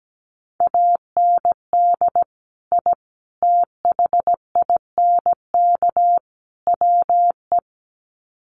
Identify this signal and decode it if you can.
Morse code, heard as a keyed tone, about 17 wpm, 700 Hz. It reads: AND I THINK WE
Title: Sets of 4 Words 17wpm